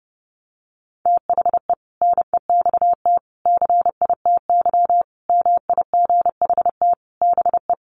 Morse code, heard as a keyed tone, about 30 wpm, 700 Hz. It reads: THE NEXT CITY MIGHT BE
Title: Sentences from Top 1400 Words 30wpm